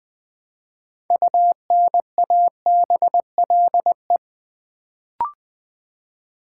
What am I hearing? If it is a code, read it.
Morse code, heard as a keyed tone, about 20 wpm, 700 Hz. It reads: UNABLE
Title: Build A Word - 3 Letters At A Time - 6-Letter Words - 20wpm